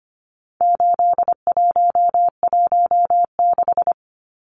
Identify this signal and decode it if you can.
Morse code, heard as a keyed tone, about 25 wpm, 700 Hz. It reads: 8116
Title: Numbers - 4-Digit 25wpm